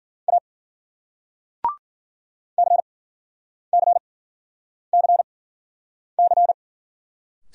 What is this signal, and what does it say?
Morse code, heard as a keyed tone, about 50 wpm, spaced as for 9 wpm, 700 Hz. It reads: A CCCC